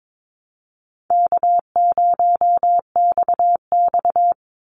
Morse code, heard as a keyed tone, about 22 wpm, 700 Hz. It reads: K0XX
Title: Easy Call Signs Repeated 3x - 22wpm